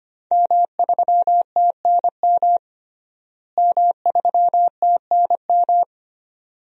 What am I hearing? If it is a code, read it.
Morse code, heard as a keyed tone, about 25 wpm, 700 Hz. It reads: M3TNM M3TNM
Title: DXCC-100 Call Signs - 5 Characters - Repeated 2x - 25wpm